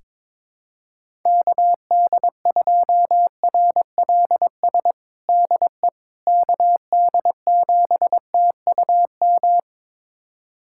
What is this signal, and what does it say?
Morse code, heard as a keyed tone, about 22 wpm, 700 Hz. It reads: KD2RLS DE KD7TUM
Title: USA Callsign DE Callsign - 6 Characters - 22wpm